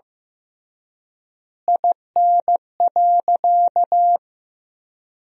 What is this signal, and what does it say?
Morse code, heard as a keyed tone, about 15 wpm, 700 Hz. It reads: IN.
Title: 3x Word-Spacing - Sentences from Top 300 Words - 15wpm